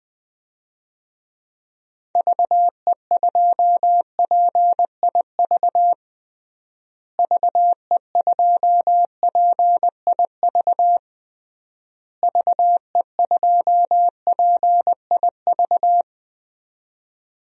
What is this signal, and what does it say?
Morse code, heard as a keyed tone, about 20 wpm, 700 Hz. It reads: VE2PIV VE2PIV VE2PIV
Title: DXCC-100 Call Signs - 6 Characters - Repeated 3x - 20wpm